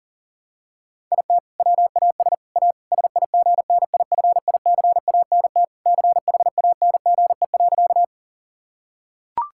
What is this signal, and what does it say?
Morse code, heard as a keyed tone, about 40 wpm, 700 Hz. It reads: IT WAS A SIGNIFICANT CHANGE.
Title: Sentences from Top 1000 Words - Review Part 6 - 40wpm